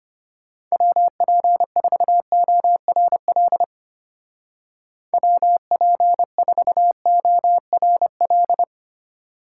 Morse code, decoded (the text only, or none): WP4ORL WP4ORL